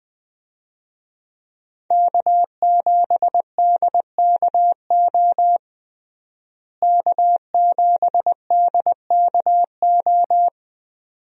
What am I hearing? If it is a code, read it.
Morse code, heard as a keyed tone, about 21 wpm, 700 Hz. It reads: K7DKO K7DKO
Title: USA Call Signs - 5 Characters - Repeated 2x - 20wpm